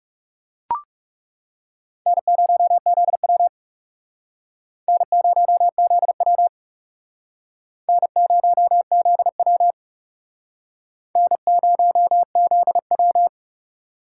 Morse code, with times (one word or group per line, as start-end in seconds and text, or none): 2.1-3.5 N0ZW
4.9-6.5 N0ZW
7.9-9.7 N0ZW
11.2-13.3 N0ZW